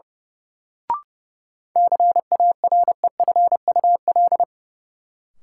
Morse code, heard as a keyed tone, about 30 wpm, 700 Hz. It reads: CAREFUL
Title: Sentences from Top 1600 Words - New Word Drill - 30wpm